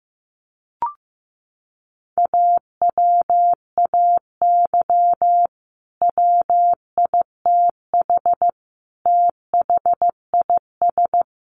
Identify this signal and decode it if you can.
Morse code, heard as a keyed tone, about 15 wpm, 700 Hz. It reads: AWAY WITH THIS
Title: Sentences from Top 1100 Words 15wpm